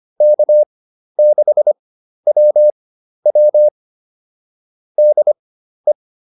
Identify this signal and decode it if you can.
Morse code, heard as a keyed tone, about 25 wpm, spaced as for 12 wpm, 600 Hz. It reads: K6WW DE
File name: QSO 204 12